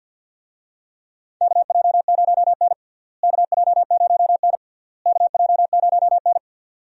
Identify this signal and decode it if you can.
Morse code, heard as a keyed tone, about 48 wpm, 700 Hz. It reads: KJ0N KJ0N KJ0N
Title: USA Call Signs - 4 Characters - Repeated 3x - 50wpm